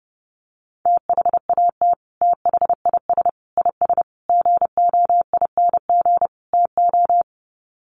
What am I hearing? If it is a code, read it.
Morse code, heard as a keyed tone, about 30 wpm, 700 Hz. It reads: THAT THIS IS GOING TO
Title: Sets of 5 Words Encore2 30wpm